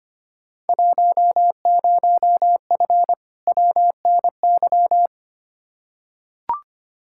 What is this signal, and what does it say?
Morse code, heard as a keyed tone, about 25 wpm, 700 Hz. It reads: 10F WNY
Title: Field Day Exchanges Encore - 25wpm